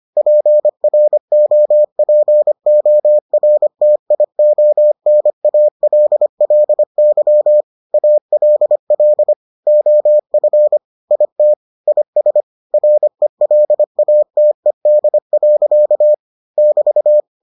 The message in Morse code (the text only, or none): PROPORTIONALLY ALL OF IT IS RELATED. <BT>